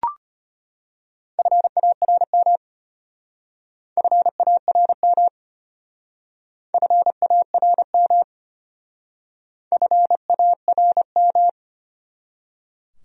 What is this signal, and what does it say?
Morse code, heard as a keyed tone, about 30 wpm, 700 Hz. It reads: FARM FARM FARM FARM
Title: Speed-Racing Top 500 Words 25wpm